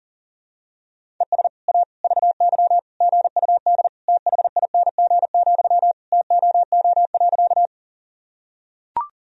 Morse code, heard as a keyed tone, about 40 wpm, 700 Hz. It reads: ES A VY GUD THING, TOO.
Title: Sentences from Top 200 Words and Abbreviations 40wpm